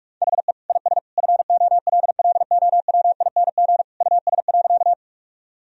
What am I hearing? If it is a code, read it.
Morse code, heard as a keyed tone, about 46 wpm, 700 Hz. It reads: HE IS FOLLOWING US.
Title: Sentences from Top 1000 Words - Review Part 4 - 45wpm